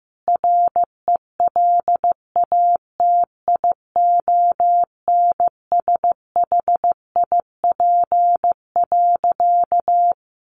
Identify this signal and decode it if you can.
Morse code, heard as a keyed tone, about 15 wpm, 700 Hz. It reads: RELATIONSHIP.
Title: Sentences from Top 1100 Words 15wpm